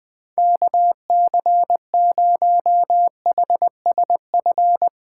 Morse code, heard as a keyed tone, about 20 wpm, 700 Hz. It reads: KC0HSF
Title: Speed Racing - USA Call Signs 20wpm